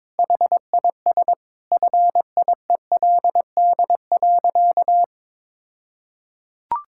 Morse code, heard as a keyed tone, about 22 wpm, 700 Hz. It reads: HIS FIELD.
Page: morse-code-ninja-practice-22wpm